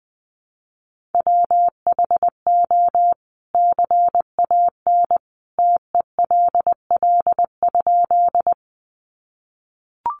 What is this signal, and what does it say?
Morse code, heard as a keyed tone, about 20 wpm, 700 Hz. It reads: WHO CAN TELL?
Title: A Sound Of Thunder 20wpm